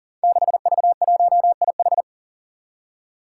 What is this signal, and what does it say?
Morse code, heard as a keyed tone, about 40 wpm, 700 Hz. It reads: 6V1IH
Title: Five Challenging Letter-Number Combinations 40wpm